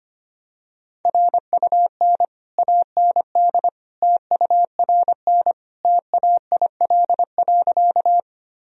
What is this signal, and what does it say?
Morse code, heard as a keyed tone, about 25 wpm, 700 Hz. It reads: RUN AND TURN TAIL.
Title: Sentences from Top 1000 Words - Review Part 3 - 25wpm